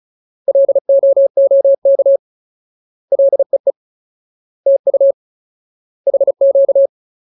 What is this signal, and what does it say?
Morse code, heard as a keyed tone, about 34 wpm, 550 Hz. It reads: LOOK LEE TU HQ
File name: Ultimate Word List / 35WPM Ultimate